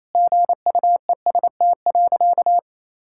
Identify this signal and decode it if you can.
Morse code, heard as a keyed tone, about 27 wpm, 700 Hz. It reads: GUEST.
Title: Sentences from Top 1600 Words 28wpm